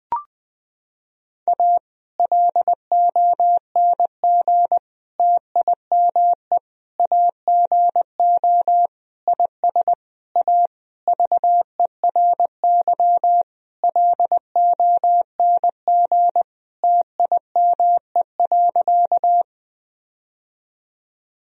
Morse code, heard as a keyed tone, about 20 wpm, 700 Hz. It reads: A LONG TIME AGO IS A VERY LONG TIME.
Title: Sentences from Top 1000 Words - Review Part 3 - 20wpm